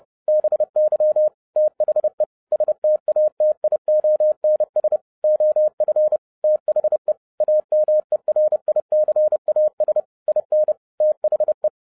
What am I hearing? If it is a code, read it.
Morse code, heard as a keyed tone, about 30 wpm, 610 Hz. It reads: BY THE STATIONS OF THE AMERICAS IN THE
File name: 2012-01-mp3 / n12-I DIDNT MISS THE SOLAR CYCLE 24-30wpm